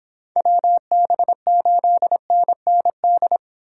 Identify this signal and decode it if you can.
Morse code, heard as a keyed tone, about 24 wpm, 700 Hz. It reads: WB8NND